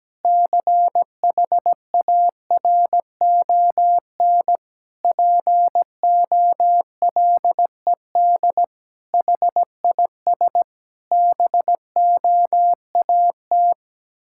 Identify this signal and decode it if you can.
Morse code, heard as a keyed tone, about 17 wpm, 700 Hz. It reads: CHARON POLED HIS BOAT
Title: Pangram Challenge v2 - 17wpm